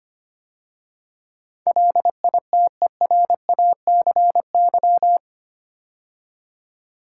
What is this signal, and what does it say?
Morse code, heard as a keyed tone, about 25 wpm, 690 Hz. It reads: LITERACY
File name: 8 Letter Words - First 3 Letters Given - 25wpm